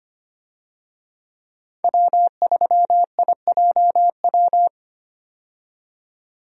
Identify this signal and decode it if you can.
Morse code, heard as a keyed tone, about 25 wpm, 690 Hz. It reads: W3IJW
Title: USA Call Signs - 5 Characters - 25wpm